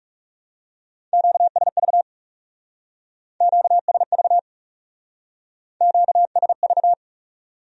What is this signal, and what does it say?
Morse code, heard as a keyed tone, about 39 wpm, 700 Hz. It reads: QSV QSV QSV